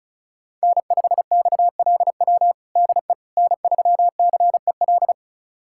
Morse code, heard as a keyed tone, about 34 wpm, 700 Hz. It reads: N5XLW DE N3CEL
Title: USA Callsign DE Callsign 35wpm